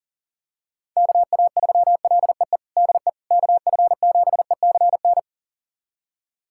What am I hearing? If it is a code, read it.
Morse code, heard as a keyed tone, about 40 wpm, 700 Hz. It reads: KA3LEE DE KF7ECN
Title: USA Callsign DE Callsign - 6 Characters - 40wpm